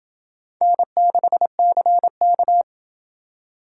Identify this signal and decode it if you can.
Morse code, heard as a keyed tone, about 28 wpm, 700 Hz. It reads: N6CK